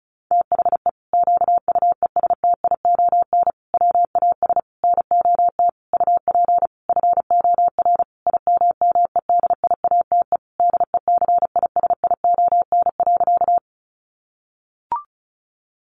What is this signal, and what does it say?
Morse code, heard as a keyed tone, about 34 wpm, 700 Hz. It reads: THE QUESTION WAS NOT UP FOR IMMEDIATE DECISION.
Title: Sentences from Top 1200 Words 35wpm